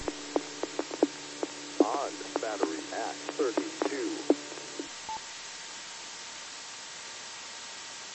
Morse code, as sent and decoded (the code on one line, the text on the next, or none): none